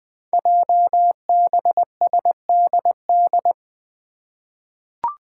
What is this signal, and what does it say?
Morse code, heard as a keyed tone, about 21 wpm, 700 Hz. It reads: JBSDD
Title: Five Challenging Letter-Number Combinations 20wpm